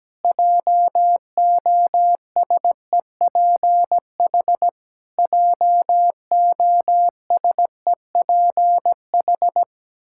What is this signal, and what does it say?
Morse code, heard as a keyed tone, about 17 wpm, 700 Hz. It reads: JOSEPH JOSEPH
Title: K1USN SST QSO Practice 17wpm